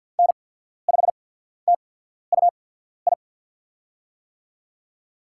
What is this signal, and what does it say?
Morse code, heard as a keyed tone, about 54 wpm, spaced as for 14 wpm, 700 Hz. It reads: N5TUI
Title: USA Call Signs - 5 Characters - 8x Character Spacing - 50wpm